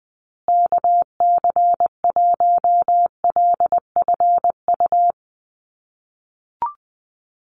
Morse code, heard as a keyed tone, about 20 wpm, 700 Hz. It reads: KC1LFU
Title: USA Call Signs Repeated 2x 20wpm